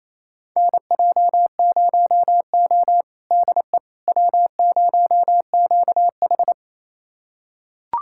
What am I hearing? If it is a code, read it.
Morse code, heard as a keyed tone, about 28 wpm, 700 Hz. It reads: NJ0O DE W0QH
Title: USA Callsign DE Callsign - 4 Characters - 28wpm